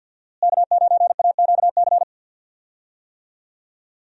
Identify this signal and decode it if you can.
Morse code, heard as a keyed tone, about 51 wpm, 700 Hz. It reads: K9AQC
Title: USA Call Signs - 5 Characters - Repeated 2x - 50wpm